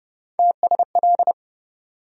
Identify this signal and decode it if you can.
Morse code, heard as a keyed tone, about 30 wpm, 700 Hz. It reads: TSL
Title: USA Call Sign Postfixes - 3 Letters 30wpm